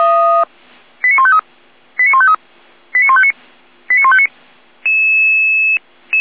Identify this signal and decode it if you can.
Five-tone selective call (ZVEI), heard as quick runs of five tones, tones of about 70 ms. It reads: 89153 89153 89159 89159
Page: ZVEI